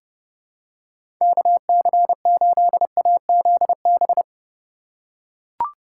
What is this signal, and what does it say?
Morse code, heard as a keyed tone, about 30 wpm, 700 Hz. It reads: KC8AZB
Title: USA Call Signs - 8x Character Spacing - 30wpm